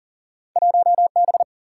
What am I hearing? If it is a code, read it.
Morse code, heard as a keyed tone, about 38 wpm, 700 Hz. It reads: 1 B
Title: Single-Letter Single-Number 40wpm